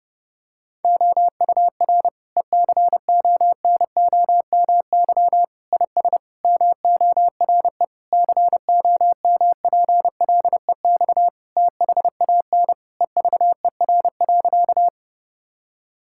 Morse code, heard as a keyed tone, about 30 wpm, 700 Hz. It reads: OUR ECONOMY IS MORE COMPLEX THAN EVER.
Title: Sentences from Top 1000 Words - Review Part 6 - 30wpm